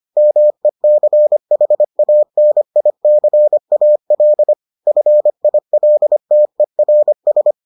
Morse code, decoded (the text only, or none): MECHANICAL FILTERS